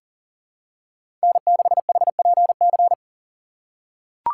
Morse code, heard as a keyed tone, about 38 wpm, 700 Hz. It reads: N6HPC
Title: USA Call Signs - 5 Characters - 4x Character Spacing - 40wpm